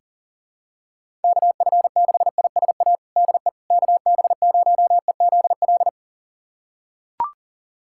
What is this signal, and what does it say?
Morse code, heard as a keyed tone, about 37 wpm, 700 Hz. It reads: KF6ISA DE KB0EZL
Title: USA Callsign DE Callsign - 6 Characters - 40wpm